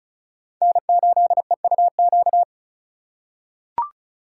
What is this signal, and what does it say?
Morse code, heard as a keyed tone, about 34 wpm, 700 Hz. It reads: N8EUQ
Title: USA Call Signs - 5 Characters - 4x Character Spacing - 35wpm